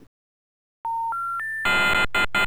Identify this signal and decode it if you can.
Special information tone, with noise.